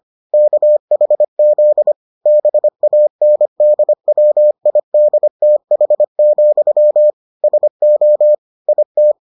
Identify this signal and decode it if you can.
Morse code, heard as a keyed tone, about 25 wpm, 600 Hz. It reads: KHZ BANDWIDTH, SO IT